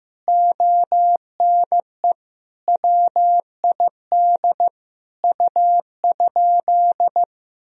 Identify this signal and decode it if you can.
Morse code, heard as a keyed tone, about 16 wpm, 700 Hz. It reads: ONE WID U?
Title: Sentences from Top 500 Words and Abbreviations 15wpm